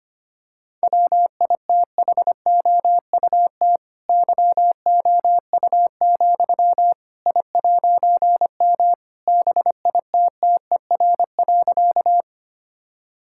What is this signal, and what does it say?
Morse code, heard as a keyed tone, about 25 wpm, 700 Hz. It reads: WITHOUT YOU, I'M BITTER.
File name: practice / 25wpm / All-Age Appropriate Jokes - 25wpm